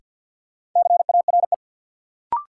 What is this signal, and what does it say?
Morse code, heard as a keyed tone, about 53 wpm, 700 Hz. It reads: CARE
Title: Sets of 3 Words - New Word Drill - 50wpm